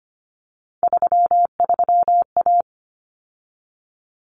Morse code, heard as a keyed tone, about 25 wpm, 700 Hz. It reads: 33A
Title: Common Cut Numbers - 3-Digit 25wpm